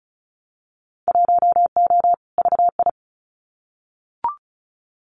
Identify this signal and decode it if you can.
Morse code, heard as a keyed tone, about 36 wpm, 700 Hz. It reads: 1O VI
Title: Winter Field Day Exchanges Encore 35wpm